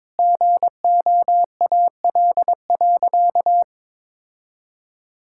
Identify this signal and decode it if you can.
Morse code, heard as a keyed tone, about 22 wpm, 700 Hz. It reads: GOAL.
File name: Sentences with 7-Word Phrases - 22wpm